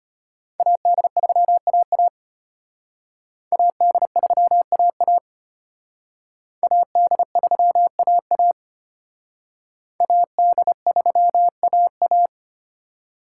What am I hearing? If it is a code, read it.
Morse code, heard as a keyed tone, about 31 wpm, 700 Hz. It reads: AD3AA AD3AA AD3AA AD3AA
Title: Speed Racing - USA 2x2 Call Signs 25wpm